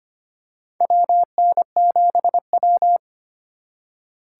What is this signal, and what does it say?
Morse code, heard as a keyed tone, about 25 wpm, 700 Hz. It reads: WN7W
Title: USA Call Signs - 4 Characters - 4x Character Spacing - 25wpm